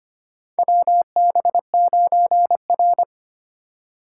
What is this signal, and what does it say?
Morse code, heard as a keyed tone, about 25 wpm, 700 Hz. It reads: WB9R
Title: USA Call Signs - 4 Characters - 4x Character Spacing - 25wpm